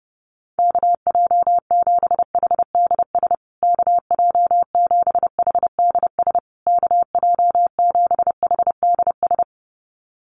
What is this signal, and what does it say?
Morse code, heard as a keyed tone, about 28 wpm, 700 Hz. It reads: KJ7HDS KJ7HDS KJ7HDS
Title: USA Call Signs Repeated 3x 30wpm